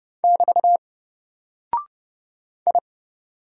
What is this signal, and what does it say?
Morse code, heard as a keyed tone, about 30 wpm, 700 Hz. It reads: <BT> I
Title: Single Character (Letter:Number:Punctuation:Prosign) Alt 30wpm